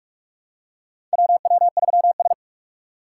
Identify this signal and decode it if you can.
Morse code, heard as a keyed tone, about 46 wpm, 700 Hz. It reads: WW3S
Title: USA Call Signs - 4 Characters - 2x Character Spacing - 45wpm